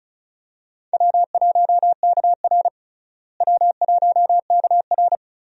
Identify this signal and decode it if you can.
Morse code, heard as a keyed tone, about 34 wpm, 700 Hz. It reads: W1KR W1KR
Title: USA Call Signs - 4 Characters - Repeated 3x - 35wpm